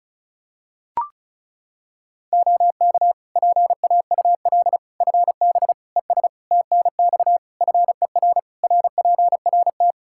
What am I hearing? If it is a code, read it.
Morse code, heard as a keyed tone, about 36 wpm, 700 Hz. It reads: OK PAUL FB ES TNX FER RPRT
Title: Practice QSO - Exchange 2 - Info - 35wpm